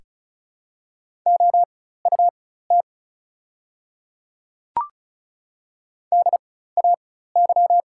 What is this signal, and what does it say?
Morse code, heard as a keyed tone, about 36 wpm, spaced as for 16 wpm, 700 Hz. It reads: OUT DAY
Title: 4x Character Spacing - Top 100 Words - 35wpm